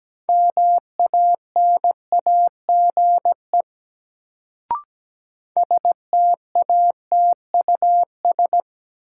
Morse code, heard as a keyed tone, about 17 wpm, 700 Hz. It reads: MANAGE STATUS
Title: Sentences from Top 1400 Words - New Word Drill - 17wpm